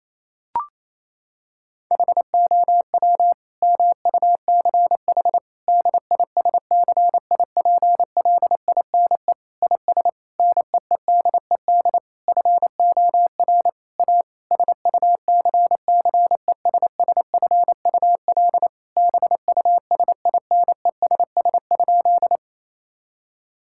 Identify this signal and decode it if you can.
Morse code, heard as a keyed tone, about 28 wpm, 700 Hz. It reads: HOW MUCH DISCIPLINE IS NEEDED FOR A SUCCESSFUL BUSINESS?
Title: Sentences from Top 1000 Words - Review Part 6 - 28wpm